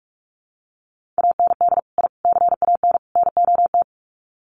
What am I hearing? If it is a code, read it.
Morse code, heard as a keyed tone, about 44 wpm, 700 Hz. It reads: AND I CAN NOT
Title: Sets of 4 Words 45wpm